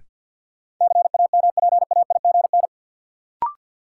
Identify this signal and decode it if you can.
Morse code, heard as a keyed tone, about 50 wpm, 700 Hz. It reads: CAMPAIGN
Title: Sentences from Top 1200 Words - New Word Drill Alt - 50wpm